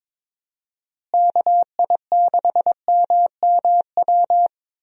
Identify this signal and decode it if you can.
Morse code, heard as a keyed tone, about 22 wpm, 700 Hz. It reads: KI6MMW
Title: USA Call Signs - 6 Characters - Repeated 2x - 22wpm